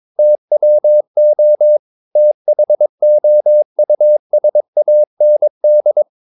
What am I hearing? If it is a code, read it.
Morse code, heard as a keyed tone, about 22 wpm, 600 Hz. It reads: TWO THOUSAND
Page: Romanen Never af Ken Follet på engelsk.